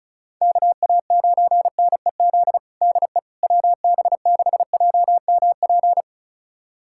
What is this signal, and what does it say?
Morse code, heard as a keyed tone, about 34 wpm, 700 Hz. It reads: KA9NEZ DE WB6JMP
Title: USA Callsign DE Callsign 35wpm